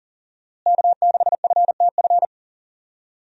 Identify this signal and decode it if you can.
Morse code, heard as a keyed tone, about 40 wpm, 700 Hz. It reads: K6FTF
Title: USA Call Signs - 5 Characters - Repeated 3x - 40wpm